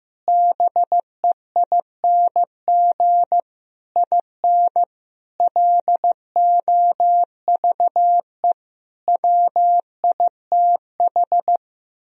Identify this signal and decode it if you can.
Morse code, heard as a keyed tone, about 16 wpm, 700 Hz. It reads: BEING IN LOVE WITH
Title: Sentences from Top 1000 Words - Review Part 3 - 15wpm